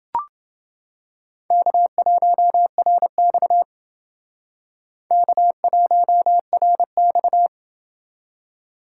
Morse code, heard as a keyed tone, about 28 wpm, 700 Hz. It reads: K1RX K1RX